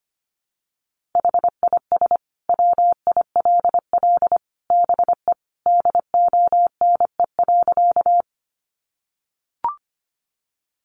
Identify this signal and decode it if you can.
Morse code, heard as a keyed tone, about 25 wpm, 700 Hz. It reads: HIS WILL BE DONE.
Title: Sentences from Top 500 Words 25wpm